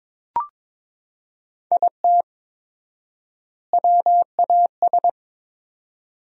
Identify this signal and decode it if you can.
Morse code, heard as a keyed tone, about 22 wpm, 700 Hz. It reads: IT WAS
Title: 4x Word-Spacing - Sentences from Top 500 Words - 22wpm